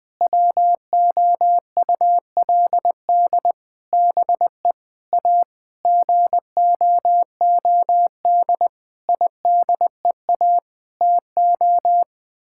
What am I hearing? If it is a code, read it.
Morse code, heard as a keyed tone, about 21 wpm, 700 Hz. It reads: WOULD BE A GOOD IDEA TO
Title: Sets of 9 Words - 20wpm